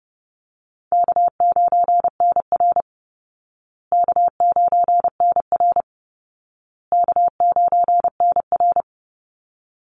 Morse code, heard as a keyed tone, about 30 wpm, 700 Hz. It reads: K9NR K9NR K9NR
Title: Easy Call Signs Repeated 3x - 30wpm